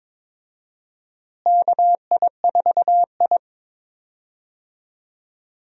Morse code, heard as a keyed tone, about 22 wpm, 700 Hz. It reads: KI4I
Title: USA Call Signs - 4 Characters - Repeated 2x - 22wpm